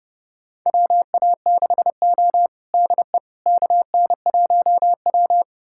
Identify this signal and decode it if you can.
Morse code, heard as a keyed tone, about 32 wpm, 700 Hz. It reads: WA6O DE KN1W
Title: USA Callsign DE Callsign - 4 Characters - 30wpm